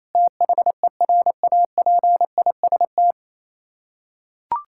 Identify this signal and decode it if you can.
Morse code, heard as a keyed tone, about 28 wpm, 700 Hz. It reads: THERAPIST
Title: Practice QSO - Exchange 3 - More Info - 28wpm